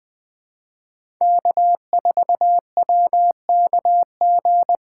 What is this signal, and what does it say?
Morse code, heard as a keyed tone, about 20 wpm, 700 Hz. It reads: K4WKG
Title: USA Call Signs - 5 Characters - 3x Character Spacing - 20wpm